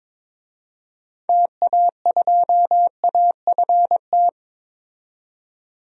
Morse code, heard as a keyed tone, about 21 wpm, 700 Hz. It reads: TA2AFT
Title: DXCC-100 Call Signs - 6 Characters - 22wpm